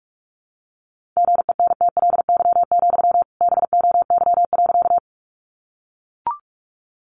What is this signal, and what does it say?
Morse code, heard as a keyed tone, about 44 wpm, 700 Hz. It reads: GENTLY, BOY.
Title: Sentences from Top 1500 Words 45wpm